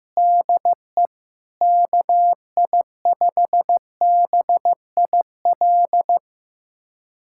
Morse code, heard as a keyed tone, about 16 wpm, 700 Hz. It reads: DE KI5BIL
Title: USA Callsign DE Callsign - 6 Characters - 15wpm